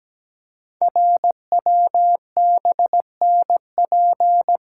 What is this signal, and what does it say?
Morse code, heard as a keyed tone, about 17 wpm, 700 Hz. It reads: RWBNP
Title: Code Groups - 5 Letters - 17wpm